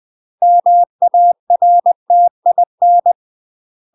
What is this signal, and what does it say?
Morse code, heard as a keyed tone, about 20 wpm, 700 Hz. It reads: MARTIN
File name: UK MASTER QSOs TRAINING / 20WPM / UK MASTER QSOs 0023